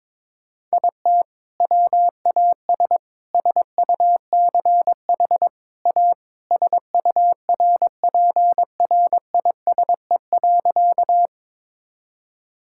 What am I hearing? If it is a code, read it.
Morse code, heard as a keyed tone, about 22 wpm, 700 Hz. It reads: IT WAS SUCH A SURPRISE.
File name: practice / 22wpm / Sentences with 2-Word Phrases - 22wpm